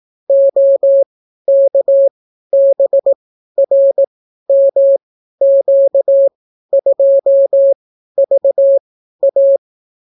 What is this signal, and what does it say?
Morse code, heard as a keyed tone, about 18 wpm, spaced as for 12 wpm, 550 Hz. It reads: OKBRMQ2VA